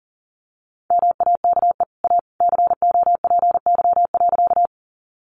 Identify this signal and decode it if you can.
Morse code, heard as a keyed tone, about 40 wpm, 700 Hz. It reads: MAKE A COPY.
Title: Sentences from Top 1400 Words 40wpm